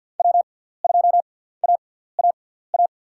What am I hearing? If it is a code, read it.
Morse code, heard as a keyed tone, about 53 wpm, spaced as for 17 wpm, 690 Hz. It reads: W2AAA
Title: USA Call Signs - 5 Characters - 6x Character Spacing - 50wpm